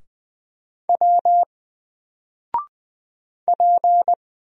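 Morse code, heard as a keyed tone, about 20 wpm, 700 Hz. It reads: W P
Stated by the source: Single letter and number Morse drill at 20 wpm, from N4LCD.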